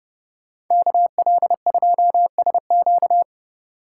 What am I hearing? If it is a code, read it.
Morse code, heard as a keyed tone, about 34 wpm, 700 Hz. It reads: KL2SQ